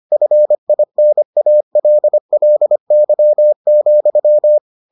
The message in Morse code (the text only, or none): FINALLY,